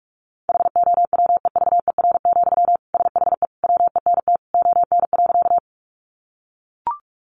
Morse code, heard as a keyed tone, about 44 wpm, 700 Hz. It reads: HOWEVER, SHE WENT ON.
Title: Sentences from Top 600 Words 45wpm